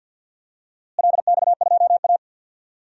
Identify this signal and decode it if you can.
Morse code, heard as a keyed tone, about 51 wpm, 700 Hz. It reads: LX2A
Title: DXCC-100 Call Signs - 4 Characters - Repeated 2x - 50wpm